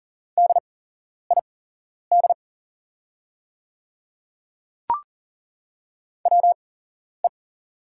Morse code, heard as a keyed tone, about 40 wpm, spaced as for 11 wpm, 690 Hz. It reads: DID WE
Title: 8x Character Spacing - Top 300 Words - 40wpm